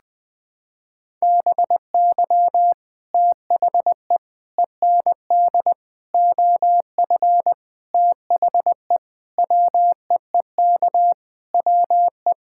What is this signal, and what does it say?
Morse code, heard as a keyed tone, about 20 wpm, 700 Hz. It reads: BY THE END OF THE WEEK WE
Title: Sentences with 5-Word Phrases - 20wpm